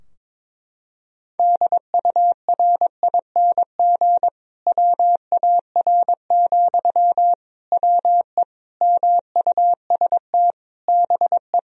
Morse code, heard as a keyed tone, about 22 wpm, 700 Hz. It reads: DURING WAR, WE MUST BE
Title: Sentences from Top 1000 Words - Review Part 4 - 22wpm